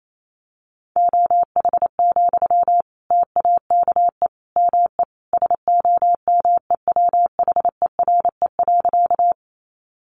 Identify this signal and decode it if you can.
Morse code, heard as a keyed tone, about 28 wpm, 700 Hz. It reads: OH, TAKE ME SOMEWHERE.
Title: Sentences from Top 1400 Words 28wpm